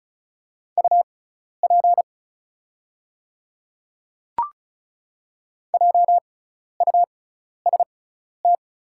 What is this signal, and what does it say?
Morse code, heard as a keyed tone, about 36 wpm, spaced as for 12 wpm, 690 Hz. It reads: UP JUST
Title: 6x Character Spacing - Top 300 Words - 35wpm